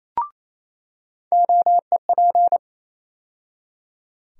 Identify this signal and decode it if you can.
Morse code, heard as a keyed tone, about 28 wpm, 700 Hz. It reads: OEP
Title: USA Call Sign Postfixes - 3 Letters 28wpm